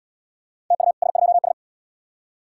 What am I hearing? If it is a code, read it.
Morse code, heard as a keyed tone, about 74 wpm, 700 Hz. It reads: TU S2S